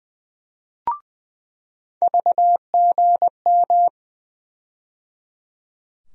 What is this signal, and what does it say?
Morse code, heard as a keyed tone, about 20 wpm, 700 Hz. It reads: VGM
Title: USA Call Sign Postfixes - 3 Letters 20wpm